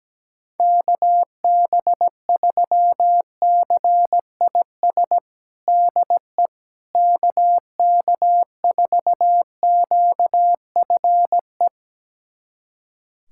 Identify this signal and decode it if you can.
Morse code, heard as a keyed tone, about 17 wpm, 700 Hz. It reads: KB3CIS DE KK4QFE
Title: USA Callsign DE Callsign - 6 Characters - 17wpm